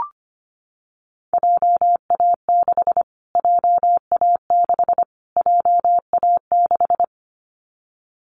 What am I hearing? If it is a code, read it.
Morse code, heard as a keyed tone, about 25 wpm, 700 Hz. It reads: JA6 JA6 JA6
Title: Callsign Prefixes for Top 25 DXCC - Repeated 3x - 25wpm